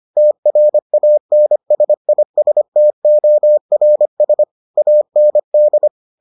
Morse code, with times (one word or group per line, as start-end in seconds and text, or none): 0.2-4.4 TRANSISTORS
4.8-5.9 AND